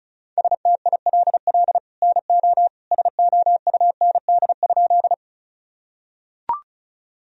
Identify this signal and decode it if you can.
Morse code, heard as a keyed tone, about 36 wpm, 700 Hz. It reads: STILL NO SOUND?
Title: Sentences from Top 1000 Words - Review Part 1 - 35wpm